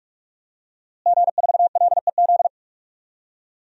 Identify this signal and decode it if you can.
Morse code, heard as a keyed tone, about 47 wpm, 700 Hz. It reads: G4LEZ
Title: DXCC-100 Call Signs - 5 Characters - Repeated 2x - 45wpm